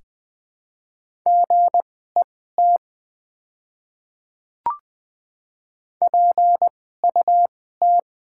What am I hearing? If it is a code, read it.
Morse code, heard as a keyed tone, about 20 wpm, spaced as for 15 wpm, 700 Hz. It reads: GET PUT
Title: 2x Character Spacing - Top 300 Words - 20wpm